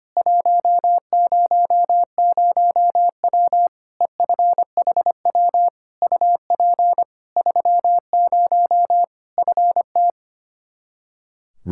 Morse code, heard as a keyed tone, about 25 wpm, 700 Hz. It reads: 100W EFHW UP 30 FT
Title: K1USN SST QSO Practice 25wpm